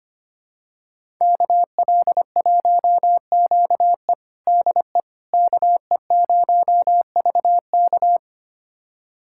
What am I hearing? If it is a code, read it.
Morse code, heard as a keyed tone, about 25 wpm, 700 Hz. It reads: KL1QE DE KE0VK
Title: USA Callsign DE Callsign - 5 Characters - 25wpm